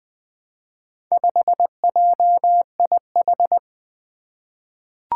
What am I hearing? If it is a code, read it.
Morse code, heard as a keyed tone, about 20 wpm, 700 Hz. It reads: 5JIH
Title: Four Challenging Letter-Number Combinations 20wpm